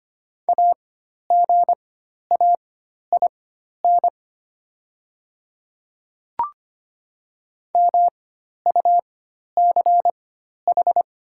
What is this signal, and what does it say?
Morse code, heard as a keyed tone, about 25 wpm, spaced as for 12 wpm, 700 Hz. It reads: AGAIN MUCH
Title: 4x Character Spacing - Top 200 Words - 25wpm